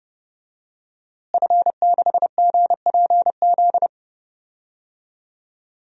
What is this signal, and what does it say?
Morse code, heard as a keyed tone, about 31 wpm, 700 Hz. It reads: F6GPZ
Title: DXCC-100 Call Signs - 5 Characters - Repeated 3x - 30wpm